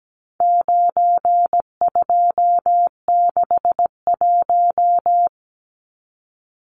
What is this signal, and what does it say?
Morse code, heard as a keyed tone, about 17 wpm, 700 Hz. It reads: 9261
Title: Numbers - 4-Digit 17wpm